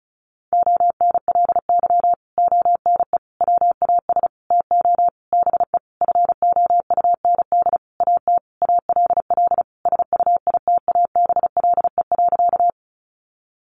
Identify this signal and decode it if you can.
Morse code, heard as a keyed tone, about 34 wpm, 700 Hz. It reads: ONLY ONE WAS TO BE FOUND AT ALL SUITABLE.
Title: Sentences from Top 1400 Words 35wpm